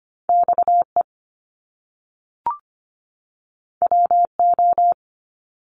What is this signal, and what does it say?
Morse code, heard as a keyed tone, about 25 wpm, 700 Hz. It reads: XE WO
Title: Top Two Letter Combinations in Words 25wpm